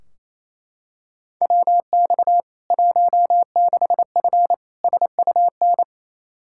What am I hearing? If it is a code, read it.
Morse code, heard as a keyed tone, about 27 wpm, 700 Hz. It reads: WX 16F SUN
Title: K1USN SST QSO Practice 28wpm